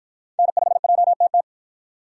Morse code, heard as a keyed tone, about 53 wpm, 700 Hz. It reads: N5JTT